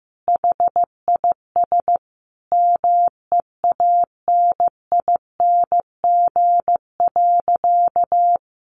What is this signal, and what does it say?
Morse code, heard as a keyed tone, about 15 wpm, 700 Hz. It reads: HIS MEANING.
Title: Sentences from Top 1100 Words 15wpm